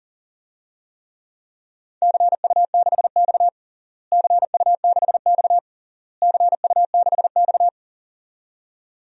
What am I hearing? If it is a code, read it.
Morse code, heard as a keyed tone, about 40 wpm, 700 Hz. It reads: CU6X CU6X CU6X
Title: DXCC-100 Call Signs - 4 Characters - Repeated 3x - 40wpm